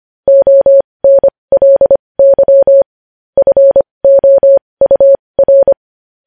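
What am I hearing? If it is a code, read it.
Morse code, heard as a keyed tone, about 25 wpm, spaced as for 20 wpm, 550 Hz. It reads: ONLY FOUR
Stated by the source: Morse code practice sending news headlines at 20 wpm.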